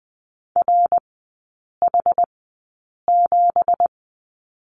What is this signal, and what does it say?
Morse code, heard as a keyed tone, about 20 wpm, spaced as for 8 wpm, 700 Hz. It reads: RH7
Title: 178 Lesson 29e - Slash - Three Characters 20wpm